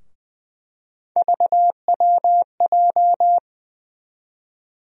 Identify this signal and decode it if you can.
Morse code, heard as a keyed tone, about 20 wpm, 700 Hz. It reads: VWJ
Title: USA Call Sign Postfixes - 3 Letters 20wpm